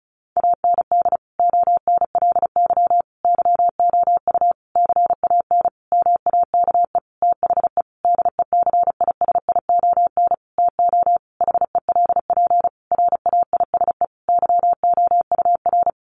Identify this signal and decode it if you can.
Morse code, heard as a keyed tone, about 34 wpm, 700 Hz. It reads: AND ONLY YOU CAN MAKE THE DECISION TO HELP RAISE YOUR
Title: Sentences from Top 1400 Words 35wpm